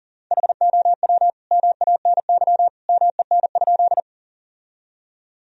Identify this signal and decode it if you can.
Morse code, heard as a keyed tone, about 37 wpm, 700 Hz. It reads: HOW MANY MEN?
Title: Sentences from Top 1000 Words - Review Part 1 - 40wpm